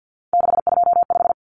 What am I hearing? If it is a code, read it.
Morse code, heard as a keyed tone, about 50 wpm, 700 Hz. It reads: 625
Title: Numbers - 3-Digit 50wpm